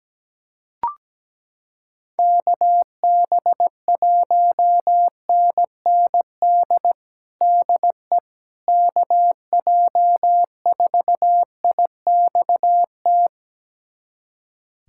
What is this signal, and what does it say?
Morse code, heard as a keyed tone, about 17 wpm, 700 Hz. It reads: KB1NND DE KJ4IXT
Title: USA Callsign DE Callsign - 6 Characters - 17wpm